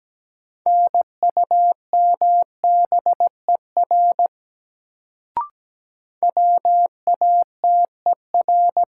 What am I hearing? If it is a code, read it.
Morse code, heard as a keyed tone, about 17 wpm, 700 Hz. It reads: NUMBER WATER
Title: Sentences from Top 100 Words - New Word Drill - 17wpm